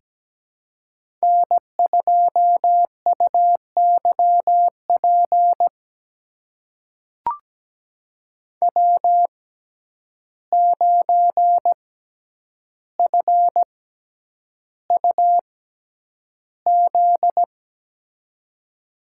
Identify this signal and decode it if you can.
Morse code, heard as a keyed tone, about 17 wpm, spaced as for 9 wpm, 700 Hz. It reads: N2UYP W9FUZ
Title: USA Call Signs - 5 Characters - 6x Character Spacing - 17wpm